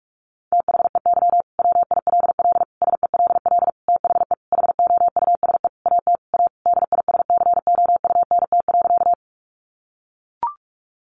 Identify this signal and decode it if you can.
Morse code, heard as a keyed tone, about 44 wpm, 700 Hz. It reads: THEY WILL SELL THE HOUSE AT A DISCOUNT.
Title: Sentences from Top 1200 Words 45wpm